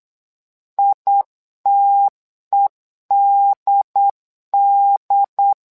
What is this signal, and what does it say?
Morse code, heard as a keyed tone, about 8 wpm, 800 Hz. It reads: ITEDD
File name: LEZIONI / 2 UHDN / 2 UHDN 8wpm2232